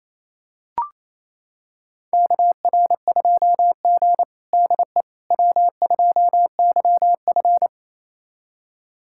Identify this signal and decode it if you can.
Morse code, heard as a keyed tone, about 28 wpm, 700 Hz. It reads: KR2G DE W2YF
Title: USA Callsign DE Callsign - 4 Characters - 28wpm